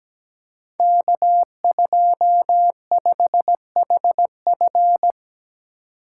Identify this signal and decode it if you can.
Morse code, heard as a keyed tone, about 17 wpm, 700 Hz. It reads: K25HF
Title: Code Groups - 5 Letter:Numbers - 17wpm